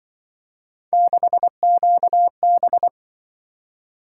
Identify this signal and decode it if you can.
Morse code, heard as a keyed tone, about 25 wpm, 700 Hz. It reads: 6QB